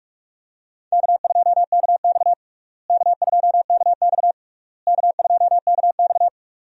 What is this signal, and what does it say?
Morse code, heard as a keyed tone, about 43 wpm, 700 Hz. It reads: K2KX K2KX K2KX
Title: USA Call Signs - 4 Characters - Repeated 3x - 45wpm